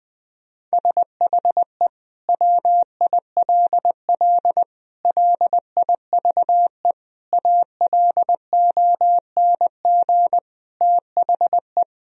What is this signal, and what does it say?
Morse code, heard as a keyed tone, about 20 wpm, 700 Hz. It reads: SHE WILL LIVE ALONG THE
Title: Sentences from Top 1000 Words - Review Part 4 - 20wpm